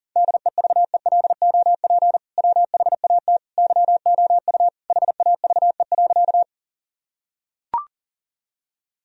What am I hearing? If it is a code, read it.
Morse code, heard as a keyed tone, about 40 wpm, 700 Hz. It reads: DEVELOP WHAT YOU HAVE.
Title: Sentences from Top 1000 Words - Review Part 2 - 40wpm